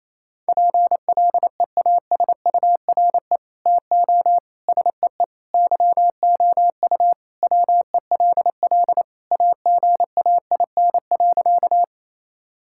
Morse code, heard as a keyed tone, about 28 wpm, 700 Hz. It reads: PLEASURE TO SEE YOU WELL AGAIN.
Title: Sentences from Top 1600 Words 28wpm